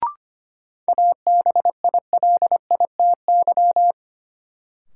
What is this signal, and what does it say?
Morse code, heard as a keyed tone, about 25 wpm, 700 Hz. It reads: ABILITY
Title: Sentences from Top 1200 Words - New Word Drill Alt - 25wpm